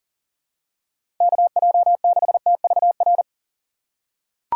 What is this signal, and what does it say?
Morse code, heard as a keyed tone, about 40 wpm, 700 Hz. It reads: KJ6TVR
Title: USA Call Signs - 6 Characters - 6x Character Spacing - 40wpm